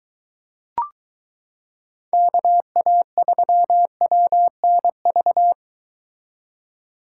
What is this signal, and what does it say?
Morse code, heard as a keyed tone, about 23 wpm, 700 Hz. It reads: KA3WNV